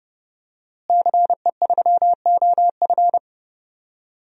Morse code, heard as a keyed tone, about 30 wpm, 700 Hz. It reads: CE3OF
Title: DXCC-100 Call Signs - 5 Characters - Repeated 2x - 30wpm